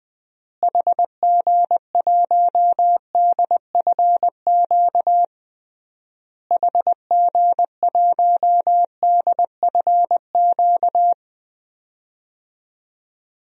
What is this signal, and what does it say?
Morse code, heard as a keyed tone, about 20 wpm, 700 Hz. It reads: HG1DFQ HG1DFQ
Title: DXCC-100 Call Signs - 6 Characters - Repeated 3x - 20wpm